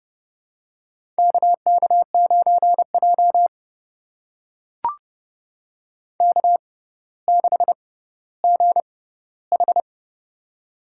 Morse code, heard as a keyed tone, about 30 wpm, spaced as for 15 wpm, 700 Hz. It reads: KK9J K6GH
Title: USA Call Signs - 4 Characters - 6x Character Spacing - 30wpm